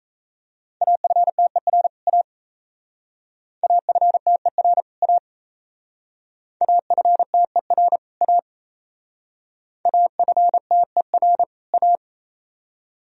Morse code, heard as a keyed tone, about 34 wpm, 700 Hz. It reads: AFTER A AFTER A AFTER A AFTER A
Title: Speed-Racing - Sets of 2 Words 28wpm